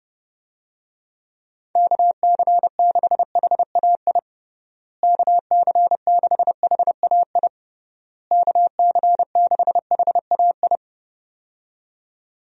Morse code, heard as a keyed tone, about 30 wpm, 700 Hz. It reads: KC6HAI KC6HAI KC6HAI
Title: USA Call Signs - 6 Characters - Repeated 3x - 30wpm